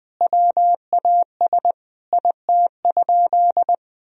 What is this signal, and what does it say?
Morse code, heard as a keyed tone, about 20 wpm, 700 Hz. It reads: WAS IT?
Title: Sentences from Top 1000 Words - Review Part 2 - 20wpm